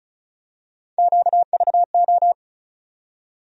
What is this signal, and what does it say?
Morse code, heard as a keyed tone, about 36 wpm, 700 Hz. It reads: QVO
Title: USA Call Sign Postfixes - 3 Letters 35wpm